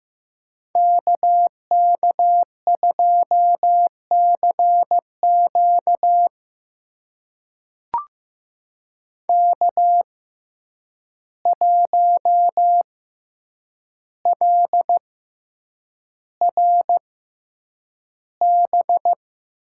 Morse code, heard as a keyed tone, about 15 wpm, spaced as for 8 wpm, 700 Hz. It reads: KK2CQ K1LRB
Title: USA Call Signs - 5 Characters - 6x Character Spacing - 15wpm